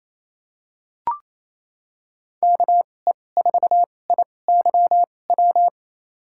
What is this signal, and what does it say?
Morse code, heard as a keyed tone, about 28 wpm, spaced as for 20 wpm, 700 Hz. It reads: KE4IYW
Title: USA Call Signs - 6 Characters - 2x Character Spacing - 28wpm